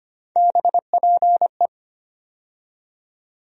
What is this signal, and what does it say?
Morse code, heard as a keyed tone, about 25 wpm, 700 Hz. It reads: BPE
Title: USA Call Sign Postfixes - 3 Letters 25wpm